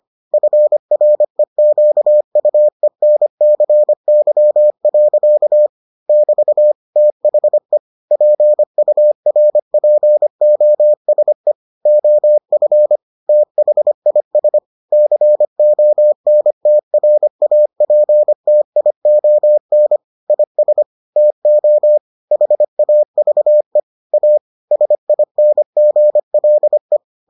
Morse code, heard as a keyed tone, about 25 wpm, 600 Hz. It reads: FREQUENCY. <BT> THE PURPOSE OF THIS CONTRAPTION IS TO HAVE A SINGLE